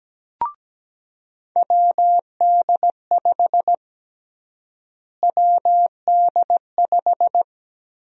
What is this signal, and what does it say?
Morse code, heard as a keyed tone, about 17 wpm, 700 Hz. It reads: WD5 WD5
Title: USA Call Sign Prefixes - Repeated 2x - 17wpm